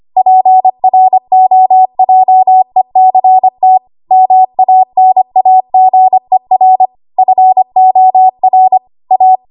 Morse code, heard as a keyed tone, about 25 wpm, 750 Hz. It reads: PROJECT MANAGER FOR A
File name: Archive / 25 WPM / 25WPM